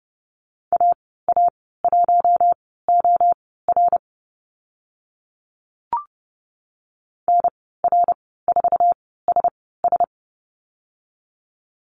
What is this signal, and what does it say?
Morse code, heard as a keyed tone, about 30 wpm, spaced as for 17 wpm, 700 Hz. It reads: AA1OR NR4SS
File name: Lesson 14g - Number 2 - Call Signs 30wpm